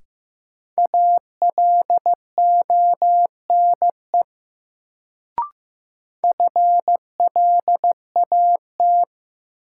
Morse code, heard as a keyed tone, about 15 wpm, 700 Hz. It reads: ALONE FLAT
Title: Sentences from Top 1000 Words - New Word Drill - 15wpm